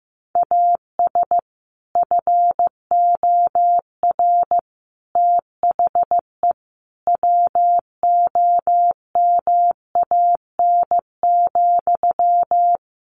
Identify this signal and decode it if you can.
Morse code, heard as a keyed tone, about 14 wpm, 700 Hz. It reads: AS FOR THE WOMAN,
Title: Sentences from Top 1100 Words 15wpm